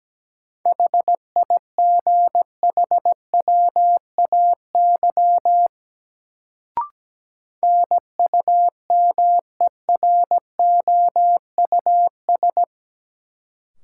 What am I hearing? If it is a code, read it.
Morse code, heard as a keyed tone, about 17 wpm, 700 Hz. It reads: HIGHWAY NUMEROUS
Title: Sentences from Top 1400 Words - New Word Drill Alt - 17wpm